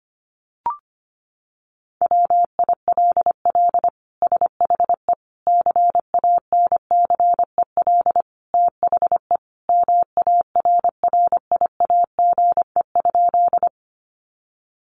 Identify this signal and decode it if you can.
Morse code, heard as a keyed tone, about 25 wpm, 700 Hz. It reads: WILL SHE CANCEL THE MARRIAGE?
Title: Sentences from Top 1200 Words 25wpm